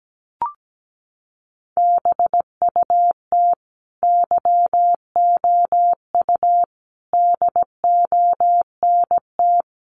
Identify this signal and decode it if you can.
Morse code, heard as a keyed tone, about 17 wpm, 700 Hz. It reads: BUT YOU DONT
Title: Sentences from Top 600 Words 17wpm